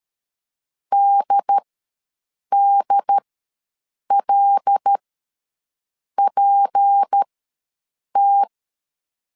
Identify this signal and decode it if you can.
Morse code, heard as a keyed tone, about 13 wpm, spaced as for 6 wpm, 800 Hz. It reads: DDLPT
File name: LEZIONI / 6 PKX0 / 6 PKX0 ws 12wpm 3448